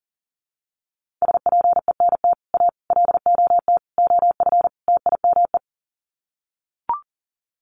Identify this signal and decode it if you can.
Morse code, heard as a keyed tone, about 37 wpm, 700 Hz. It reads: SPENT A LOT OF TIME
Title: Sets of 5 Words 40wpm